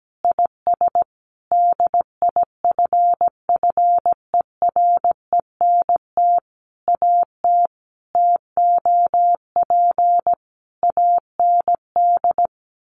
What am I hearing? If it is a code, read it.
Morse code, heard as a keyed tone, about 17 wpm, 700 Hz. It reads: IS DIFFERENT AT TOP AND
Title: Sentences from Top 1400 Words 17wpm